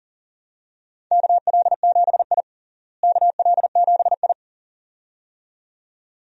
Morse code, heard as a keyed tone, about 40 wpm, 700 Hz. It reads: KL7I KL7I
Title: DXCC-100 Call Signs - 4 Characters - Repeated 2x - 40wpm